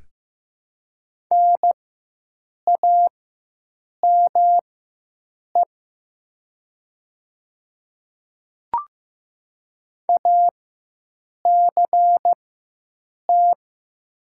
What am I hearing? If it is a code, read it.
Morse code, heard as a keyed tone, about 15 wpm, spaced as for 7 wpm, 700 Hz. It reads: NAME ACT
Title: 4x Character Spacing - Top 200 Words - 15wpm